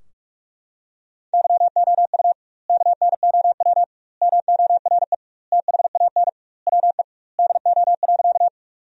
Morse code, heard as a keyed tone, about 41 wpm, 700 Hz. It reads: YOU KNOW MORE THAN WE DO.
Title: 2x Word-Spacing - Sentences from Top 100 Words - 45wpm